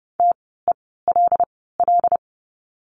Morse code, heard as a keyed tone, about 30 wpm, spaced as for 17 wpm, 700 Hz. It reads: TELL
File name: course / 30:10wpm / 115 Lesson 19f - Letter w - Words 30wpm